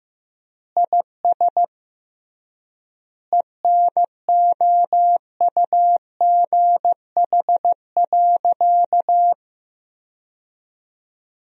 Morse code, heard as a keyed tone, about 16 wpm, 700 Hz. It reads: IS ENOUGH.
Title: 3x Word-Spacing - Sentences from Top 400 Words - 15wpm